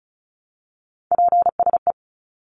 Morse code, heard as a keyed tone, about 34 wpm, 700 Hz. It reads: PSE
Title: Abbreviations 35wpm